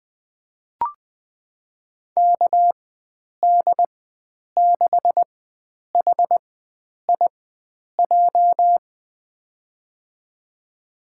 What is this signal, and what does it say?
Morse code, heard as a keyed tone, about 20 wpm, spaced as for 9 wpm, 700 Hz. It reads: KD6HIJ
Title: USA Call Signs - 6 Characters - 4x Character Spacing - 20wpm